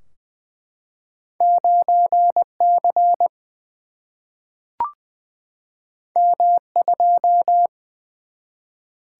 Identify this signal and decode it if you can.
Morse code, heard as a keyed tone, about 20 wpm, 700 Hz. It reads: 9 C M 2
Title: Single-Letter Single-Number 20wpm